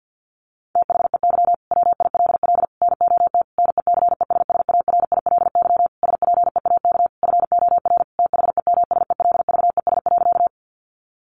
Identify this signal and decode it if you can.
Morse code, heard as a keyed tone, about 50 wpm, 700 Hz. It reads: THEY WILL NOT NECESSARILY SPEAK FOR THEMSELVES.
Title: Sentences from Top 1400 Words 50wpm